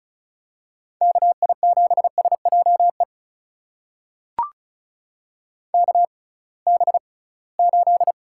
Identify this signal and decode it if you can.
Morse code, heard as a keyed tone, about 36 wpm, spaced as for 23 wpm, 700 Hz. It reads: KI7SJE KB8
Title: USA Call Signs - 6x Character Spacing - 35wpm